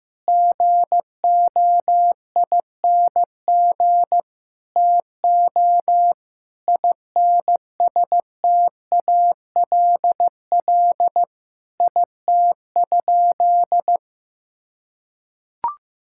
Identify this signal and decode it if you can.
Morse code, heard as a keyed tone, about 15 wpm, 700 Hz. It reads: GOING TO INSTALL IT?
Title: Sentences from Top 1600 Words 15wpm